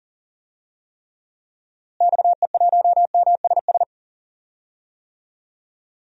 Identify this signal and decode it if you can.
Morse code, heard as a keyed tone, about 40 wpm, 700 Hz. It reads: XE1MSS
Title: DXCC-100 Call Signs - 6 Characters - Repeated 3x - 40wpm